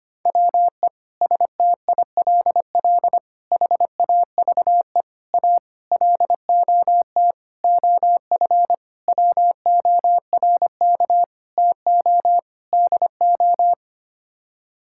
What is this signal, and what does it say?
Morse code, heard as a keyed tone, about 25 wpm, 700 Hz. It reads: WE STILL HAVE A LOT OF WORK TO DO
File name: Sets of 9 Words - 25wpm